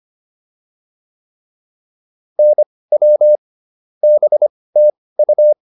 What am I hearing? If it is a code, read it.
Morse code, heard as a keyed tone, about 25 wpm, spaced as for 18 wpm, 600 Hz. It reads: NW BTU